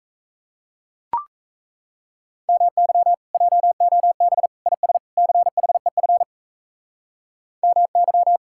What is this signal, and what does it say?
Morse code, heard as a keyed tone, about 37 wpm, 700 Hz. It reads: MY JOB IS CHEF MY